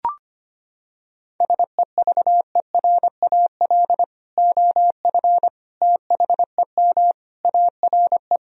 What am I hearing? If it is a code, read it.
Morse code, heard as a keyed tone, about 25 wpm, 700 Hz. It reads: SEVERAL OF THEM ARE
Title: Sentences from Top 1000 Words - Review Part 2 - 25wpm